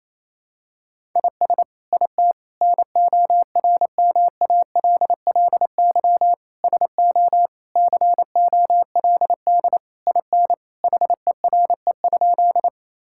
Morse code, heard as a keyed tone, about 28 wpm, 700 Hz. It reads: IS IT NORMALLY SO COLD IN HERE?
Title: Sentences from Top 1000 Words - Review Part 6 - 28wpm